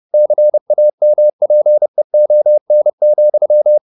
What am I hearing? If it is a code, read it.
Morse code, heard as a keyed tone, about 30 wpm, 600 Hz. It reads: CAMPEON,